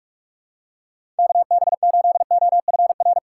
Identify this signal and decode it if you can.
Morse code, heard as a keyed tone, about 46 wpm, 700 Hz. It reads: KB8OFR
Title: USA Call Signs - 6 Characters - 3x Character Spacing - 45wpm